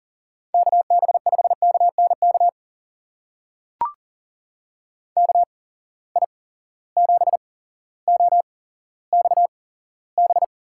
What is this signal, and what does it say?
Morse code, heard as a keyed tone, about 40 wpm, spaced as for 17 wpm, 700 Hz. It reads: KB5KNK KI7OXB
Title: USA Call Signs - 6 Characters - 8x Character Spacing - 40wpm